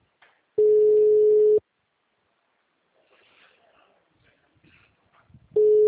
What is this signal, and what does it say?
Ringback tone (Europe, 425 Hz).